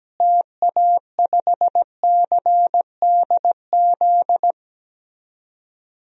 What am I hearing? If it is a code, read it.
Morse code, heard as a keyed tone, about 17 wpm, 700 Hz. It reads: TA5CDZ
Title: DXCC-100 Call Signs - 6 Characters - Repeated 2x - 17wpm